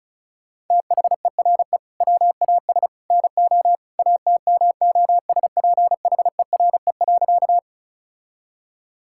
Morse code, heard as a keyed tone, about 36 wpm, 700 Hz. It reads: THERE WAS NO ATMOSPHERE.
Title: Sentences from Top 1600 Words 35wpm